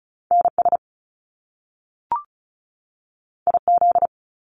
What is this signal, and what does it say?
Morse code, heard as a keyed tone, about 36 wpm, 700 Hz. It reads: NS IZ
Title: Top Two Letter Combinations in Words 35wpm